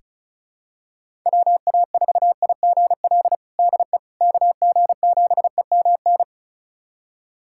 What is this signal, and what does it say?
Morse code, heard as a keyed tone, about 36 wpm, 700 Hz. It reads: WA4IGL DE KG7EMN
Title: USA Callsign DE Callsign - 6 Characters - 35wpm